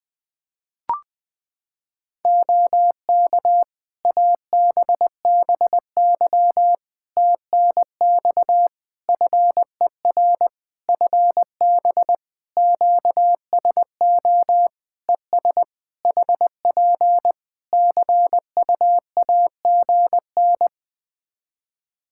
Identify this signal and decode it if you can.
Morse code, heard as a keyed tone, about 21 wpm, 700 Hz. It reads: OK ABBY TNX FER FB QSO ES HP CUAGN
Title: Practice QSO - Exchange 4 - Ending - 20wpm